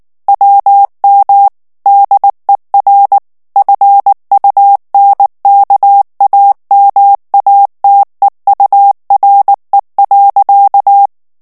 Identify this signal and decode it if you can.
Morse code, heard as a keyed tone, about 19 wpm, 800 Hz. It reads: WM DER FUNKAMATEURE.
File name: morse mp3 / ALLEMAND texte clair 2 vitesse 18 mots minute